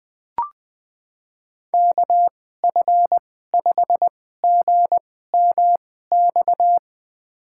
Morse code, heard as a keyed tone, about 20 wpm, spaced as for 14 wpm, 700 Hz. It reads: KF5GMX
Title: USA Call Signs - 6 Characters - 2x Character Spacing - 20wpm